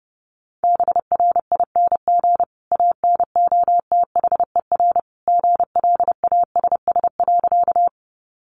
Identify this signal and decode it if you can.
Morse code, heard as a keyed tone, about 30 wpm, 700 Hz. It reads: BRING ANOTHER GLASS.
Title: Sentences from Top 1300 Words 30wpm